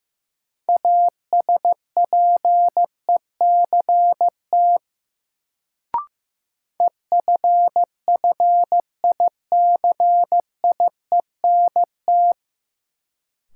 Morse code, heard as a keyed tone, about 15 wpm, 700 Hz. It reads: ASPECT EFFICIENT
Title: Sentences from Top 1300 Words - New Word Drill Alt - 15wpm